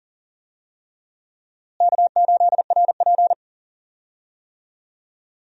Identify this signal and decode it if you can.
Morse code, heard as a keyed tone, about 37 wpm, 700 Hz. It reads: K8RP
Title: USA Call Signs - 4 Characters - Repeated 3x - 40wpm